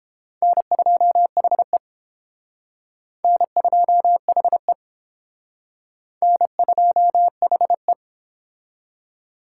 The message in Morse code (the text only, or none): N2HE N2HE N2HE